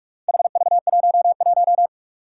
Morse code, heard as a keyed tone, about 46 wpm, 700 Hz. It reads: HV11
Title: Four Challenging Letter-Number Combinations 45wpm